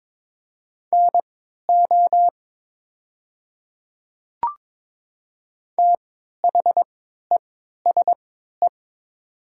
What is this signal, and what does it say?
Morse code, heard as a keyed tone, about 22 wpm, spaced as for 12 wpm, 700 Hz. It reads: NO THESE
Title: 3x Character Spacing - Top 500 Words - 22wpm